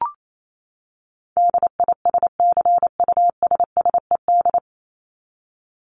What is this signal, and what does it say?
Morse code, heard as a keyed tone, about 28 wpm, 700 Hz. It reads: DISCUSSED
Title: ED Suffix - Top 100 Words with Base Words 28wpm